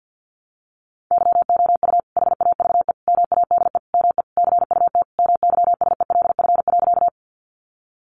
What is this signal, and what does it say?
Morse code, heard as a keyed tone, about 50 wpm, 700 Hz. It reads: YOU HAVE MADE ME CUT MYSELF.
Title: Sentences from Top 1300 Words 50wpm